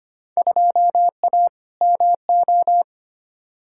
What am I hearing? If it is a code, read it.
Morse code, heard as a keyed tone, about 25 wpm, 700 Hz. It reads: 2A MO
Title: Field Day Exchanges Encore - 25wpm